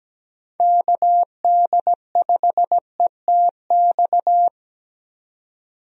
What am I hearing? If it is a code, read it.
Morse code, heard as a keyed tone, about 17 wpm, 700 Hz. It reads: KD5ETX
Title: USA Call Signs - 6 Characters - 17wpm